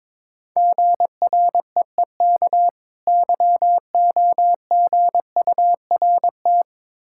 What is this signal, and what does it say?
Morse code, heard as a keyed tone, about 22 wpm, 690 Hz. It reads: GREEK YOGURT
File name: Difficult Words with Hints - 22wpm